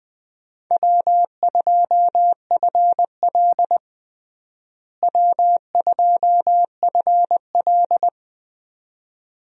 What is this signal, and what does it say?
Morse code, heard as a keyed tone, about 20 wpm, 700 Hz. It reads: W2FL W2FL
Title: USA Call Signs - 4 Characters - Repeated 2x - 20wpm